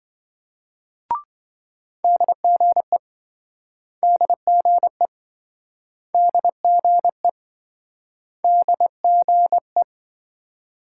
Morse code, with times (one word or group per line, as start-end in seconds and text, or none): 2.0-3.0 DGE
4.0-5.1 DGE
6.1-7.3 DGE
8.4-9.8 DGE